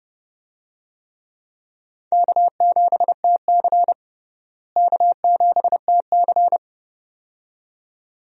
Morse code, heard as a keyed tone, about 32 wpm, 700 Hz. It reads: K7TC K7TC
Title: USA Call Signs - 4 Characters - Repeated 2x - 30wpm